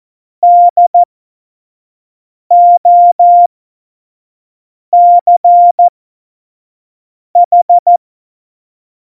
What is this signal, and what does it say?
Morse code, heard as a keyed tone, about 14 wpm, spaced as for 5 wpm, 700 Hz. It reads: DOCH
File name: Morsekurs / Witze / Witz 08. August 17, Tempo 25 F70